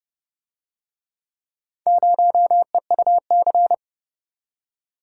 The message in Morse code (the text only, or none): 0EUC